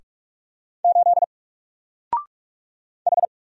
Morse code, heard as a keyed tone, about 44 wpm, 700 Hz. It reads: 8 H
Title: Single Character (Letter:Number:Punctuation:Prosign) 45wpm